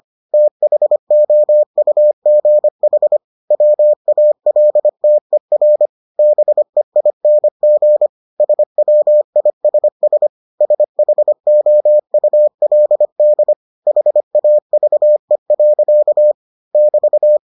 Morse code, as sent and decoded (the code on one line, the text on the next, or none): - .... --- ..- --. .... / .-- .- .-.. - . .-. / -... .. -. --. / ... .-- .. ... ... / ... .... --- ..- .-.. -.. / .... .- ...- . .-.-.- / -...-
THOUGH WALTER BEING SWISS SHOULD HAVE. <BT>